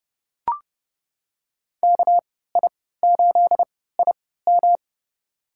Morse code, heard as a keyed tone, about 30 wpm, spaced as for 17 wpm, 700 Hz. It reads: KI8IM
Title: USA Call Signs - 5 Characters - 3x Character Spacing - 30wpm